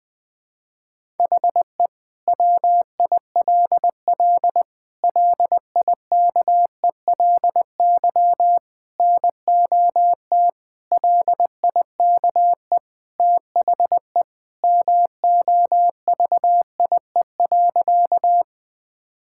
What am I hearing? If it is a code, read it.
Morse code, heard as a keyed tone, about 20 wpm, 700 Hz. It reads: HE WILL LIKELY NOT LIKE THE MOVIE.
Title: Sentences from Top 1000 Words - Review Part 5 - 20wpm